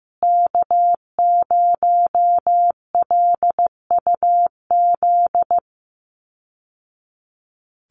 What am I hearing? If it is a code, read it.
Morse code, heard as a keyed tone, about 15 wpm, 700 Hz. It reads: K0LUZ
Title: Call Signs Repeated 2x - 15wpm